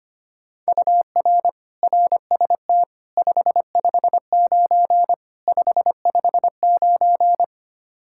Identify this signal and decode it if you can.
Morse code, heard as a keyed tone, about 25 wpm, 700 Hz. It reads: UR RST 559 559
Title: Practice QSO - Exchange 1 - Beginning - 25wpm